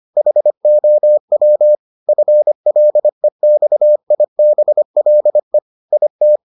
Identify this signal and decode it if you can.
Morse code, heard as a keyed tone, about 25 wpm, 600 Hz. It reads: HOW FLEXIBLE IT